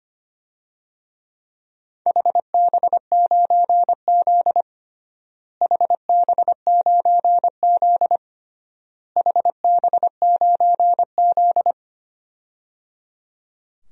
Morse code, heard as a keyed tone, about 25 wpm, 700 Hz. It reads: HB9Z HB9Z HB9Z
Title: DXCC-100 Call Signs - 4 Characters - Repeated 3x - 25wpm